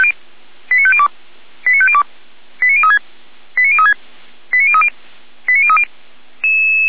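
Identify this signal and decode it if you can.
Five-tone selective call (ZVEI), heard as quick runs of five tones, tones of about 70 ms. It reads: 89502 89502 89036 89036 89030 89030
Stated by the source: ZVEI five-tone selective call 89140 received on SDR, from Feuerwehr Orlamünde (f85360).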